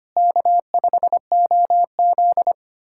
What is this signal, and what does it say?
Morse code, heard as a keyed tone, about 25 wpm, 700 Hz. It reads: K5OZ
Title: USA Callsign DE Callsign - 4 Characters - 25wpm